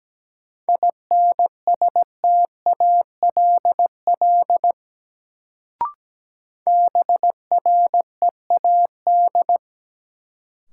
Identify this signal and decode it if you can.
Morse code, heard as a keyed tone, about 17 wpm, 700 Hz. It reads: INSTALL BREAD
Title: Sentences from Top 1600 Words - New Word Drill Alt - 17wpm